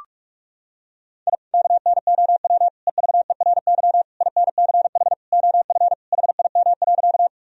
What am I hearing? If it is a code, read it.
Morse code, heard as a keyed tone, about 41 wpm, 700 Hz. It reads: I KNOW EVERY INCH OF HIM.
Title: Sentences from Top 1000 Words - Review Part 3 - 45wpm